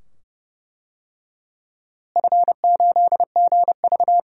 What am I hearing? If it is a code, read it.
Morse code, heard as a keyed tone, about 31 wpm, 700 Hz. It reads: F8GV
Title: DXCC-100 Call Signs - 4 Characters - Repeated 2x - 30wpm